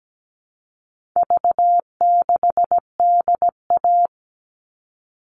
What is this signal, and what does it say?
Morse code, heard as a keyed tone, about 17 wpm, 700 Hz. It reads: V6DA
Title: All Cut Numbers - 4-Digit 17wpm